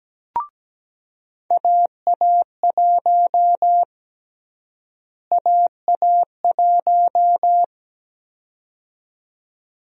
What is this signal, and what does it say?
Morse code, heard as a keyed tone, about 17 wpm, 700 Hz. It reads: AA1 AA1
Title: USA Call Sign Prefixes - Repeated 2x - 17wpm